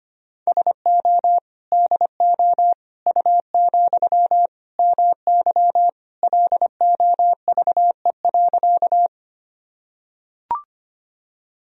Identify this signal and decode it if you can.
Morse code, heard as a keyed tone, about 25 wpm, 700 Hz. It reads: SO DO U, MY LOVE.
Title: Sentences from Top 400 Words and Abbreviations 25wpm